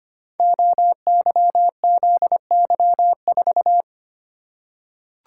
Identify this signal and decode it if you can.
Morse code, heard as a keyed tone, about 25 wpm, 700 Hz. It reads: OYZY4
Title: Code Groups - 5 Letter:Numbers - 25wpm